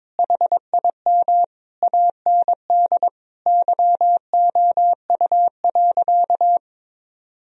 Morse code, heard as a keyed tone, about 22 wpm, 700 Hz. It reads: HIM AND YOU.
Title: Sentences from Top 1000 Words - Review Part 1 - 22wpm